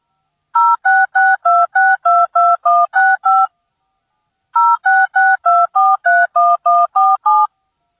DTMF, heard as telephone touch tones, tones of about 220 ms. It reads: 0662622165 0662431147